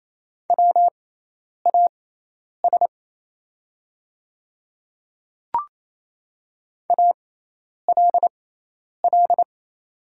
Morse code, heard as a keyed tone, about 27 wpm, spaced as for 10 wpm, 700 Hz. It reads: WAS ALL
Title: 6x Character Spacing - Top 100 Words - 28wpm